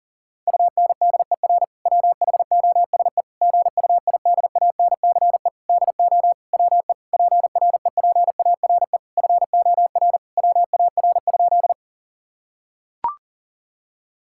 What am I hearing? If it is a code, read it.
Morse code, heard as a keyed tone, about 40 wpm, 700 Hz. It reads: UNDER WHOSE GUIDANCE DO WE PREPARE FOR WAR?
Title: Sentences from Top 1600 Words 40wpm